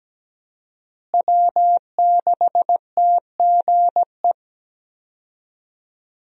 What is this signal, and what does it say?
Morse code, heard as a keyed tone, about 17 wpm, 700 Hz. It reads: W6TGE
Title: USA Call Signs - 5 Characters - Repeated 3x - 17wpm